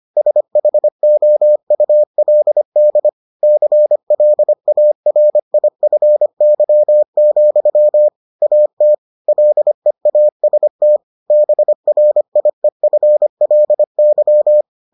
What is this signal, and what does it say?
Morse code, heard as a keyed tone, about 25 wpm, 600 Hz. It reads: SHOULD CLARIFY, AT LEAST BRIEFLY